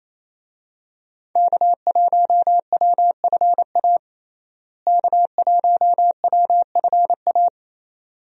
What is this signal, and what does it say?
Morse code, heard as a keyed tone, about 28 wpm, 700 Hz. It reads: K1WFA K1WFA
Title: USA Call Signs - 5 Characters - Repeated 2x - 28wpm